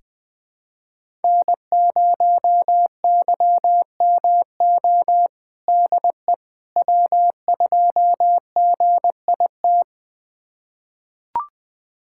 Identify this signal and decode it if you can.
Morse code, heard as a keyed tone, about 20 wpm, 700 Hz. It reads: N0YMO DE W2GIT
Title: USA Callsign DE Callsign - 5 Characters - 20wpm